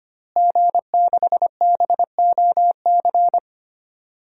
Morse code, heard as a keyed tone, about 25 wpm, 700 Hz. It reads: G6BOC
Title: DXCC-100 Call Signs - 5 Characters - Repeated 3x - 25wpm